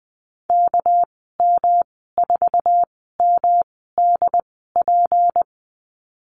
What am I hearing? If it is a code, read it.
Morse code, heard as a keyed tone, about 20 wpm, spaced as for 14 wpm, 700 Hz. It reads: KM4MDP
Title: 245 Lesson 39g - Letter z - Call Signs 20wpm